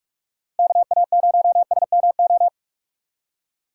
Morse code, heard as a keyed tone, about 47 wpm, 700 Hz. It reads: KA0SMO
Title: USA Call Signs - 4x Character Spacing - 45wpm